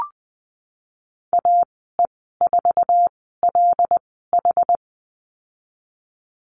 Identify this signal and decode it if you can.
Morse code, heard as a keyed tone, about 20 wpm, spaced as for 15 wpm, 700 Hz. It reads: AE4LH
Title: 179 Lesson 29g - Slash - Call Signs 20wpm